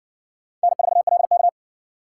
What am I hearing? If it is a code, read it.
Morse code, heard as a keyed tone, about 60 wpm, 700 Hz. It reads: N4LK